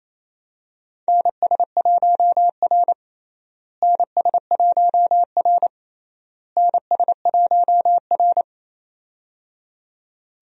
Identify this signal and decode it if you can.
Morse code, heard as a keyed tone, about 28 wpm, 700 Hz. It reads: NS1R NS1R NS1R
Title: USA Call Signs - 4 Characters - Repeated 3x - 28wpm